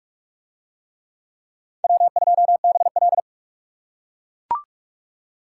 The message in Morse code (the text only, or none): W2BL